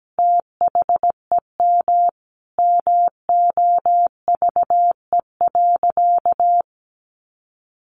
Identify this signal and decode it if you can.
Morse code, heard as a keyed tone, about 17 wpm, 700 Hz. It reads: THEM MOVE.
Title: A Sound Of Thunder 17wpm